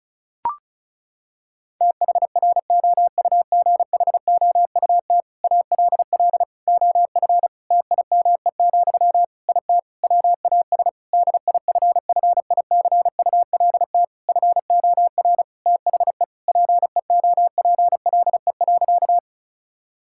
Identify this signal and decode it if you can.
Morse code, heard as a keyed tone, about 34 wpm, 700 Hz. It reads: THROUGHOUT ALL OF TIME, IT WAS DIFFICULT FOR THE PEOPLE.
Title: Sentences from Top 1000 Words - Review Part 5 - 35wpm